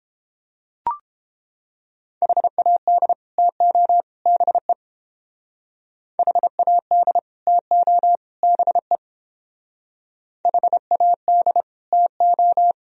Morse code, heard as a keyed tone, about 28 wpm, 700 Hz. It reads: HAD TO BE HAD TO BE HAD TO